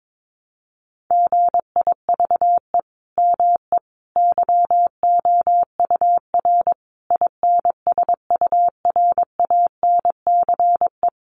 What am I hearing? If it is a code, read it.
Morse code, heard as a keyed tone, about 22 wpm, 700 Hz. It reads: GIVE ME YOUR INSURANCE
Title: Sentences from Top 1100 Words 22wpm